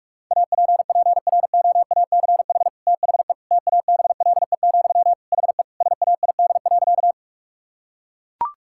Morse code, heard as a keyed tone, about 41 wpm, 700 Hz. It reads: APPROACH THE TABLE, HE SAID.
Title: Sentences from Top 1000 Words - Review Part 6 - 45wpm